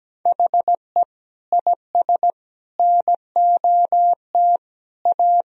Morse code, heard as a keyed tone, about 17 wpm, 700 Hz. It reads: HE IS NOT A
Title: Sentences from Top 1000 Words - Review Part 2 - 17wpm